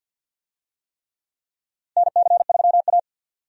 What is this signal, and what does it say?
Morse code, heard as a keyed tone, about 53 wpm, 700 Hz. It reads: NC3A
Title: USA Call Signs - 4 Characters - 50wpm